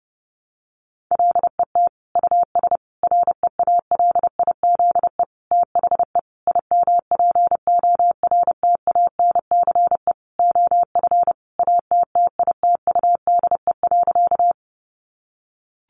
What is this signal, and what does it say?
Morse code, heard as a keyed tone, about 30 wpm, 700 Hz. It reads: LET US REALIZE THE IMPORTANCE OF ATTITUDE.
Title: Sentences from Top 1300 Words 30wpm